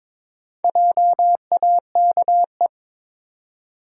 Morse code, heard as a keyed tone, about 22 wpm, 700 Hz. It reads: JAKE
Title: USA Common Names - 22wpm